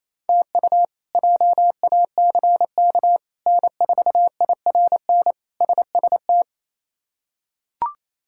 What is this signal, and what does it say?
Morse code, heard as a keyed tone, about 28 wpm, 700 Hz. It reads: TU JACK N4IRN SST
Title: K1USN SST QSO Practice - 28wpm